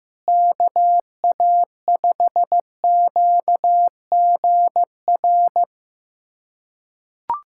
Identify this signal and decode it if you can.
Morse code, heard as a keyed tone, about 15 wpm, 700 Hz. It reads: KA5QGR
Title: USA Callsign DE Callsign 15wpm